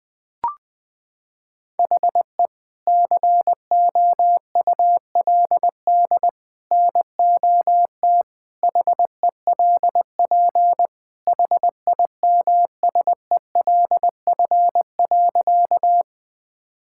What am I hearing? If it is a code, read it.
Morse code, heard as a keyed tone, about 20 wpm, 700 Hz. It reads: HE COULD NOT HELP HIMSELF.
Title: Sentences from Top 1600 Words 20wpm